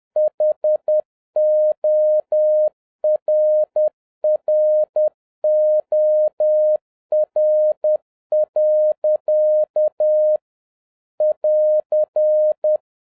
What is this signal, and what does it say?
Morse code, heard as a keyed tone, about 10 wpm, 610 Hz. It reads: HORROR. <AR>